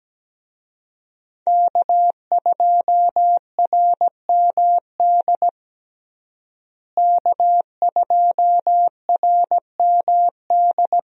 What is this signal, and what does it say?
Morse code, heard as a keyed tone, about 17 wpm, 700 Hz. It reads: K2RMD K2RMD
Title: USA Call Signs - 5 Characters - Repeated 2x - 17wpm